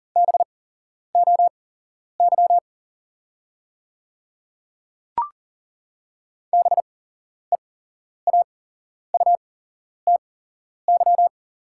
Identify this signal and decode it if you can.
Morse code, heard as a keyed tone, about 40 wpm, spaced as for 11 wpm, 690 Hz. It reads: BOY BEAUTY
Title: 8x Character Spacing - Top 500 Words - 40wpm